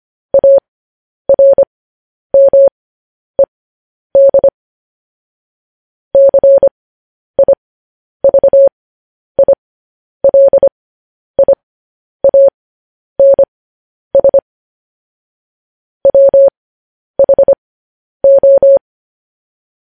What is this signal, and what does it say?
Morse code, heard as a keyed tone, about 25 wpm, spaced as for 10 wpm, 550 Hz. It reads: ARMED CIVILIANS WHO
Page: news-headlines-in-morse-code-at-10-wpm